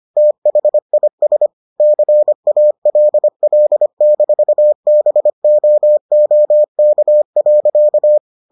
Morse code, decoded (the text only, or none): THIS CALL-BOOK.